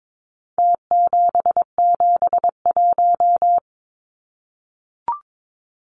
Morse code, heard as a keyed tone, about 22 wpm, 700 Hz. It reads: T771
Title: Numbers - Contest Style - 4-Digit 22wpm